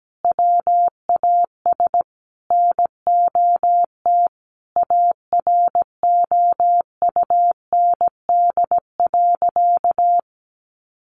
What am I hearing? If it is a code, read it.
Morse code, heard as a keyed tone, about 17 wpm, 700 Hz. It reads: WAS NOT AROUND.
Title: Sentences from Top 1300 Words 17wpm